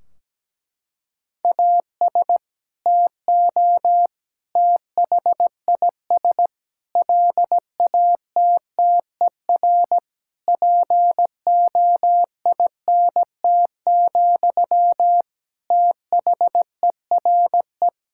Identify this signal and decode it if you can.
Morse code, heard as a keyed tone, about 17 wpm, 700 Hz. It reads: AS TO THIS LATTER POINT, THERE
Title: Sentences from Top 1600 Words 17wpm